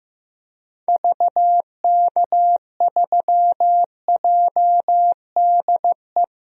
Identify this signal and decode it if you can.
Morse code, heard as a keyed tone, about 15 wpm, 700 Hz. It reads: VK3JDE
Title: DXCC-100 Call Signs - 6 Characters - 15wpm